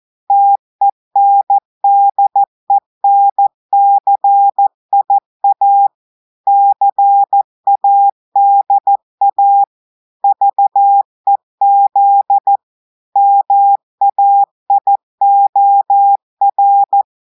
Morse code, heard as a keyed tone, about 14 wpm, 800 Hz. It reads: TENDENCIA CADA VEZ MAIOR
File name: audios / portugais / portugais texte en clair 2 vitesse 14 mots minutes